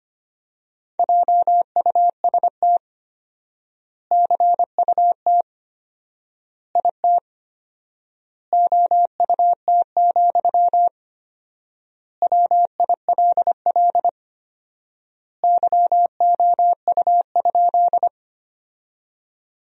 Morse code, heard as a keyed tone, about 25 wpm, 700 Hz. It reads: JUST CUT IT OUT, WILL YOU?
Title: 4x Word-Spacing - Sentences from Top 300 Words - 25wpm